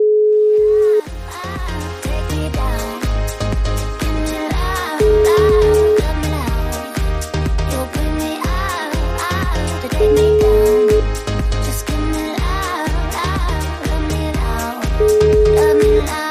Ringback tone (Europe, 425 Hz), with noise.